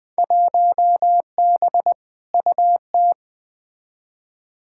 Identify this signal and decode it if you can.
Morse code, heard as a keyed tone, about 20 wpm, 700 Hz. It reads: 1B UT
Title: Field Day Exchanges - 20wpm